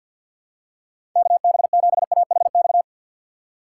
Morse code, heard as a keyed tone, about 47 wpm, 700 Hz. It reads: KB7AHX
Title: USA Call Signs - 6 Characters - 6x Character Spacing - 50wpm